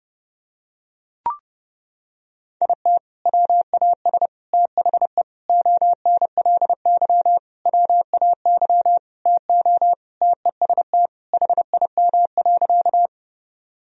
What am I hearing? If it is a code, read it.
Morse code, heard as a keyed tone, about 31 wpm, 700 Hz. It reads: IT WAS THE ONLY WAY TO TEST HIM.
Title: Sentences from Top 1000 Words - Review Part 3 - 30wpm